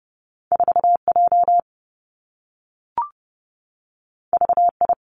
Morse code, heard as a keyed tone, about 30 wpm, 700 Hz. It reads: 4 J V I
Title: Two Challenging Letter-Number Combinations 30wpm